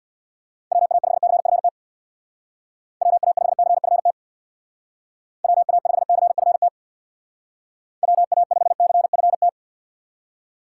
Morse code, heard as a keyed tone, about 64 wpm, 690 Hz. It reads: WA5CFT WA5CFT WA5CFT WA5CFT